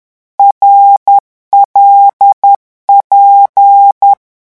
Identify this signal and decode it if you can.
Morse code, heard as a keyed tone, about 10 wpm, 800 Hz. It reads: RLP
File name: morse mp3 / lecon6-vitesse 10